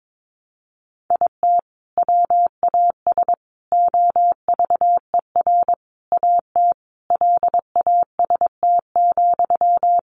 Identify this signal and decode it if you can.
Morse code, heard as a keyed tone, about 22 wpm, 700 Hz. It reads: IT WAS OVER AT LAST,
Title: Sentences from Top 600 Words 22wpm